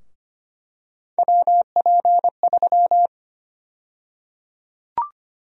Morse code, heard as a keyed tone, about 25 wpm, 700 Hz. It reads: WP3
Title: USA Call Sign Prefixes - Repeated 2x - 25wpm